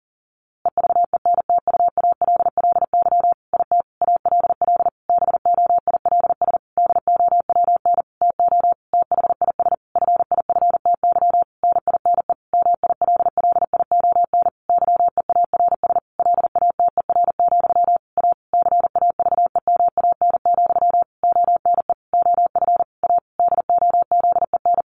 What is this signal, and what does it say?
Morse code, heard as a keyed tone, about 40 wpm, 700 Hz. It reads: EVENTUALLY IT ALL BOILS DOWN TO THIS FIFTY NINE MILLION YEARS LATER, A CAVEMAN, ONE OF A DOZEN
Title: A Sound Of Thunder 40wpm